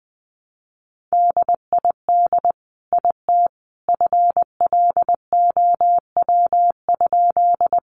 Morse code, heard as a keyed tone, about 20 wpm, 700 Hz. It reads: DID IT FLOW?
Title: Sentences from Top 1200 Words 20wpm